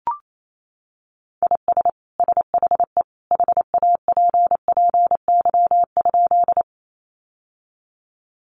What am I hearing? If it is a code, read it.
Morse code, heard as a keyed tone, about 28 wpm, 700 Hz. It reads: IS SHE HAPPY?
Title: Sentences from Top 1000 Words 28wpm